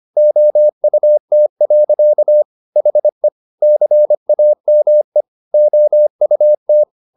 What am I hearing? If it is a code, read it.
Morse code, heard as a keyed tone, about 25 wpm, 600 Hz. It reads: OUT. HE CAME OUT
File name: the old man and the sea part 1 25wpm